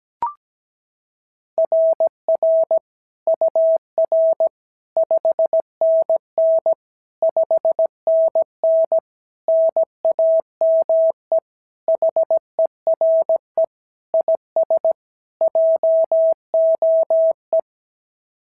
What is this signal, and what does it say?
Morse code, heard as a keyed tone, about 16 wpm, 650 Hz. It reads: RR UR 5NN 5NN NAME HERE IS JOE
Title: SOTA QSO Practice 17wpm